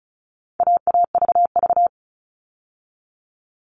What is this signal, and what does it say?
Morse code, heard as a keyed tone, about 34 wpm, 700 Hz. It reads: AAVV
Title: All Cut Numbers - 4-Digit 35wpm